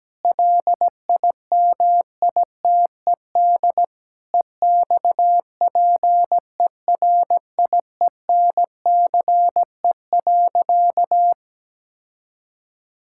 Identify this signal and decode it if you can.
Morse code, heard as a keyed tone, about 17 wpm, 700 Hz. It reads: LIMITED EXPERIENCE.
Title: Sentences from Top 1000 Words - Review Part 6 - 17wpm